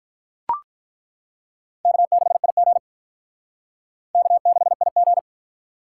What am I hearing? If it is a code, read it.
Morse code, heard as a keyed tone, about 42 wpm, 700 Hz. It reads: K6IG K6IG
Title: Speed Racing - USA 1x2 Call Signs 35wpm